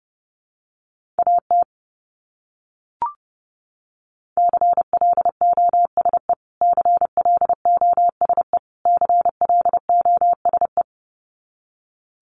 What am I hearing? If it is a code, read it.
Morse code, heard as a keyed tone, about 30 wpm, 700 Hz. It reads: AT CLOSE CLOSE CLOSE
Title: Top 500 Words Repeated 30wpm